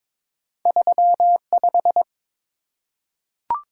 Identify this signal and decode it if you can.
Morse code, heard as a keyed tone, about 22 wpm, 700 Hz. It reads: 35
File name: practice / 22wpm / Two Letter-Number Combinations - 22wpm